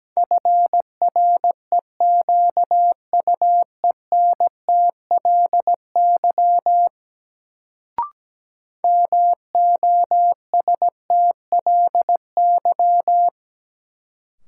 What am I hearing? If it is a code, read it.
Morse code, heard as a keyed tone, about 17 wpm, 700 Hz. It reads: FREQUENTLY MOSTLY
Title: Sentences from Top 1300 Words - New Word Drill - 17wpm